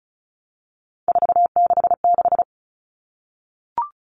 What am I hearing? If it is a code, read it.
Morse code, heard as a keyed tone, about 34 wpm, 700 Hz. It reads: 466
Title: All Cut Numbers - 3-Digit 35wpm